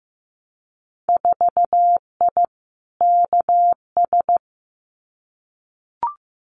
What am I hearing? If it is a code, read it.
Morse code, heard as a keyed tone, about 15 wpm, 700 Hz. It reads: 4I KS
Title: Winter Field Day Exchanges Encore 15wpm